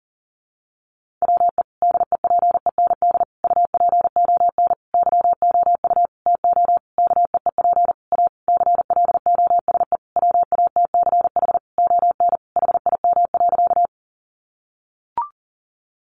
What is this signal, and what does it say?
Morse code, heard as a keyed tone, about 37 wpm, 700 Hz. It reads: WE DEPEND UPON YOU TO KEEP A CLOSE WATCH ON HIM.
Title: Sentences from Top 1300 Words 40wpm